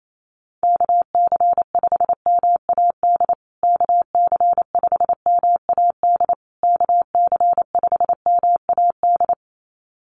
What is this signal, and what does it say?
Morse code, heard as a keyed tone, about 28 wpm, 700 Hz. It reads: KC5MAD KC5MAD KC5MAD
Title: USA Call Signs Repeated 3x 28wpm